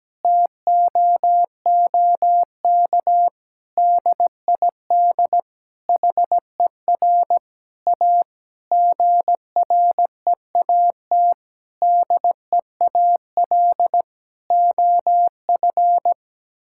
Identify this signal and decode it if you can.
Morse code, heard as a keyed tone, about 17 wpm, 700 Hz. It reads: TOOK DID HER A GREAT DEAL OF
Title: Sentences from Top 1000 Words - Review Part 6 - 17wpm